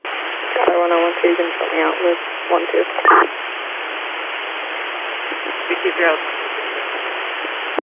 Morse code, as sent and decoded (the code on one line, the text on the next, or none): none